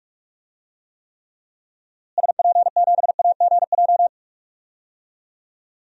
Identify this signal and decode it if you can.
Morse code, heard as a keyed tone, about 46 wpm, 700 Hz. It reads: SP7AGJ
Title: DXCC-100 Call Signs - 6 Characters - Repeated 2x - 45wpm